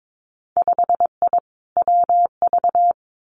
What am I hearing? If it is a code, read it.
Morse code, heard as a keyed tone, about 22 wpm, 700 Hz. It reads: 5I WV
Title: Winter Field Day Exchanges Encore 22wpm